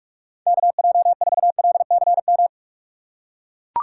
Morse code, heard as a keyed tone, about 43 wpm, 700 Hz. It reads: KJ4LCM
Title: USA Call Signs - 6 Characters - 2x Character Spacing - 45wpm